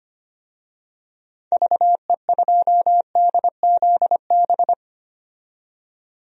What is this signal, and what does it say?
Morse code, heard as a keyed tone, about 25 wpm, 700 Hz. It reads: VE2DZB
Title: DXCC-100 Call Signs - 6 Characters - 25wpm